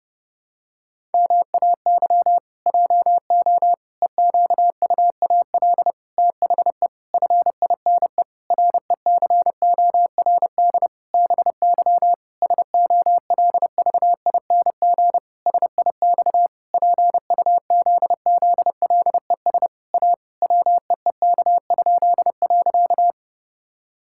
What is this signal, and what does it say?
Morse code, heard as a keyed tone, about 28 wpm, 700 Hz. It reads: MAY JO EQUAL THE FINE RECORD BY SOLVING SIX PUZZLES A WEEK?.
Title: Pangram Challenge v2 - 30wpm